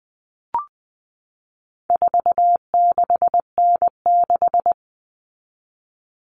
Morse code, heard as a keyed tone, about 20 wpm, 700 Hz. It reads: 46N6
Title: Common Cut Numbers - 4-Digit 20wpm